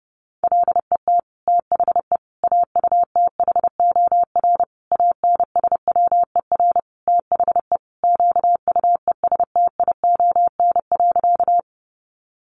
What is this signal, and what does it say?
Morse code, heard as a keyed tone, about 30 wpm, 700 Hz. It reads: LET THE AUTHOR ANSWER THE QUESTION.
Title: Sentences from Top 1200 Words 30wpm